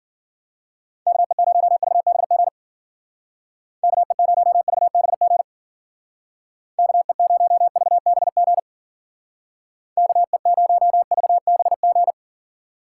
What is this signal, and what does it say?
Morse code, heard as a keyed tone, about 53 wpm, 700 Hz. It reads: KE0VBG KE0VBG KE0VBG KE0VBG